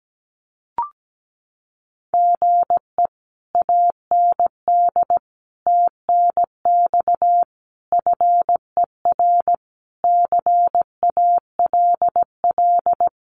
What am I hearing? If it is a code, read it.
Morse code, heard as a keyed tone, about 17 wpm, 700 Hz. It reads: GE AND TNX FER CALL
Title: Common QSO Phrases Encore 17wpm